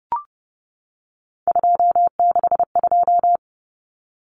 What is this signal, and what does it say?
Morse code, heard as a keyed tone, about 30 wpm, 700 Hz. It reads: 262
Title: Numbers - 3-Digit 30wpm